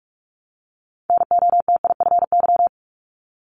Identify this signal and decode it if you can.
Morse code, heard as a keyed tone, about 46 wpm, 700 Hz. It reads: NOTIFY
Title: Multi-letter Phonemes Repeated 3x 45wpm